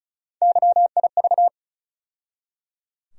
Morse code, heard as a keyed tone, about 34 wpm, 700 Hz. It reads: YIV
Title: USA Call Sign Postfixes - 3 Letters 35wpm